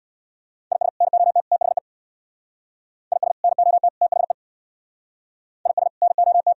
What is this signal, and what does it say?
Morse code, heard as a keyed tone, about 64 wpm, 700 Hz. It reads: IS NOT THE IS NOT THE IS NOT